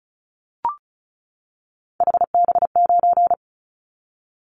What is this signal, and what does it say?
Morse code, heard as a keyed tone, about 34 wpm, 700 Hz. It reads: HB9
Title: Callsign Prefixes for Top 25 DXCC - 35wpm